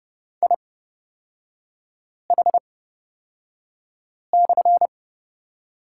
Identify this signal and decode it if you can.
Morse code, heard as a keyed tone, about 31 wpm, spaced as for 5 wpm, 690 Hz. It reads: IH/
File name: Lesson 37e - Letter x - Three Characters 30wpm